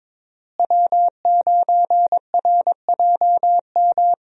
Morse code, heard as a keyed tone, about 22 wpm, 700 Hz. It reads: W9RJM
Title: USA Call Signs - 5 Characters - Repeated 2x - 22wpm